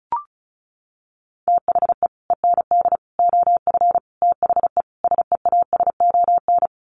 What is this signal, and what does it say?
Morse code, heard as a keyed tone, about 34 wpm, 700 Hz. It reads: THE END OF THE SEASON
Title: Sets of 5 Words Encore 35wpm